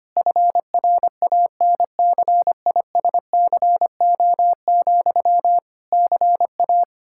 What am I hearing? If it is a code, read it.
Morse code, heard as a keyed tone, about 25 wpm, 700 Hz. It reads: FRANCISCO, CA
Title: Practice QSO - Exchange 1 - Beginning - 25wpm